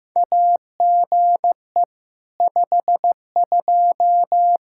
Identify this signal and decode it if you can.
Morse code, heard as a keyed tone, about 15 wpm, 700 Hz. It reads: AGE 52
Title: Practice QSO - Exchange 3 - More Info - 15wpm